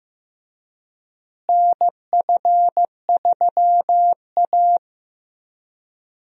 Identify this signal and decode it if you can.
Morse code, heard as a keyed tone, about 15 wpm, 700 Hz. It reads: NF3A
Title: USA Call Signs - 4 Characters - Repeated 2x - 15wpm